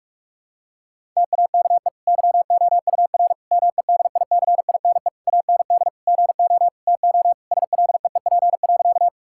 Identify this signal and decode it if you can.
Morse code, heard as a keyed tone, about 46 wpm, 700 Hz. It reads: TAKE YOUR MEDICINE AND GO TO SLEEP.
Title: Sentences from Top 1000 Words - Review Part 6 - 45wpm